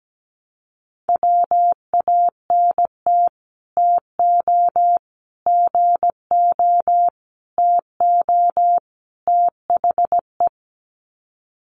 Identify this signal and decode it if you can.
Morse code, heard as a keyed tone, about 17 wpm, 700 Hz. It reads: WANT TO GO TO THE
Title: Sets of 5 Words Encore2 17wpm